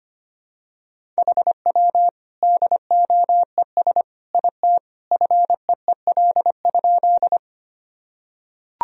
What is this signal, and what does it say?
Morse code, heard as a keyed tone, about 25 wpm, 700 Hz. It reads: HW DOES IT FEEL?
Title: Sentences from Top 400 Words and Abbreviations 25wpm